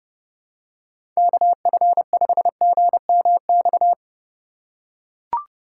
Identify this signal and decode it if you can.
Morse code, heard as a keyed tone, about 31 wpm, 700 Hz. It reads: KF5GMX
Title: USA Call Signs - 6 Characters - 6x Character Spacing - 30wpm